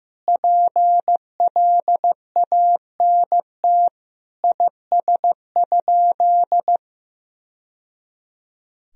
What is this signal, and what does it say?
Morse code, heard as a keyed tone, about 15 wpm, 700 Hz. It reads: PLANT IS?
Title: Sentences from Top 1000 Words - Review Part 1 - 15wpm